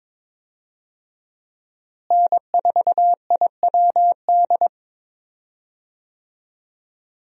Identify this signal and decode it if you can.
Morse code, heard as a keyed tone, about 21 wpm, 700 Hz. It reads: N4IWD
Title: USA Call Signs - 5 Characters - Repeated 3x - 22wpm